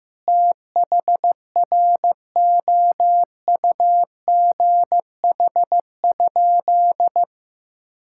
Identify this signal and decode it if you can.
Morse code, heard as a keyed tone, about 15 wpm, 700 Hz. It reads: THROUGH?
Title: Sentences from Top 1000 Words - Review Part 6 - 15wpm